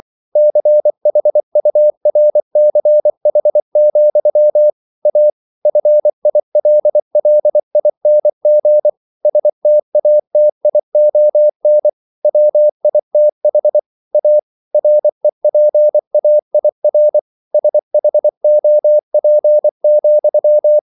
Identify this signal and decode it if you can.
Morse code, heard as a keyed tone, about 24 wpm, 600 Hz. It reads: CHURCH, A FILLING STATION WITH A REPAIR SHOP,